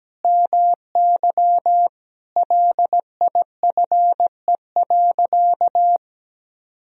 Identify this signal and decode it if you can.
Morse code, heard as a keyed tone, about 17 wpm, 700 Hz. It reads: MY LIFE.
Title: Sentences from Top 1000 Words - Review Part 3 - 17wpm